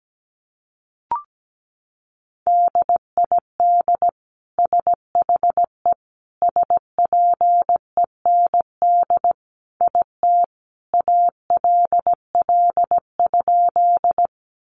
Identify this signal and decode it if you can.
Morse code, heard as a keyed tone, about 17 wpm, 700 Hz. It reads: DID SHE SPEND IT ALL?
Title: Sentences from Top 800 Words 17wpm